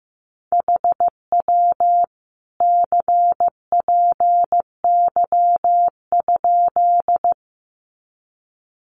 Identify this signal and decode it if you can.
Morse code, heard as a keyed tone, about 15 wpm, 700 Hz. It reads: HW CPY?
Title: Common QSO Phrases 15wpm